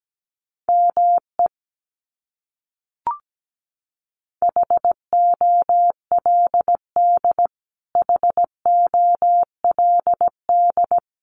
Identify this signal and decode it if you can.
Morse code, heard as a keyed tone, about 17 wpm, 700 Hz. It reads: ME HOLD HOLD
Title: Top 500 Words Repeated 17wpm